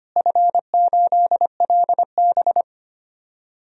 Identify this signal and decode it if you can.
Morse code, heard as a keyed tone, about 25 wpm, 700 Hz. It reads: F8LB
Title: DXCC-100 Call Signs - 4 Characters - Repeated 2x - 25wpm